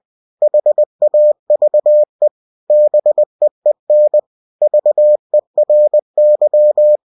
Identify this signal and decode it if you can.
Morse code, heard as a keyed tone, about 20 wpm, 600 Hz. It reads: HAVE BEEN VERY
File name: cw-books / winnie-the-pooh / winnie-20wpm-page-0096